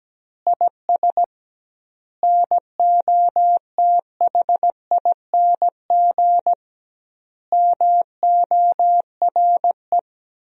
Morse code, heard as a keyed tone, about 17 wpm, 700 Hz. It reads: IS NOTHING MORE
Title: 2x Word-Spacing - Sentences from Top 500 Words - 17wpm